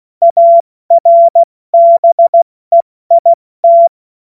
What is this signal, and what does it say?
Morse code, heard as a keyed tone, about 16 wpm, 680 Hz. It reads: ARBEIT
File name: Morsekurs / Witze / Witz 06. April 2021, 70 F80